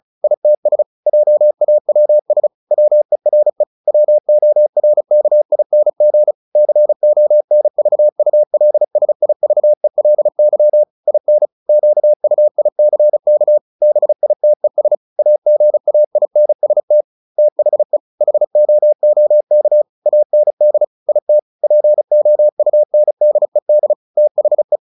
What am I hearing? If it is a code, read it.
Morse code, heard as a keyed tone, about 35 wpm, 600 Hz. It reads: ITS JAWS WERE WORKING CONVULSIVELY IN QUICK BITES AGAINST THE HOOK AND IT POUNDED THE
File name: the old man and the sea 2 35wpm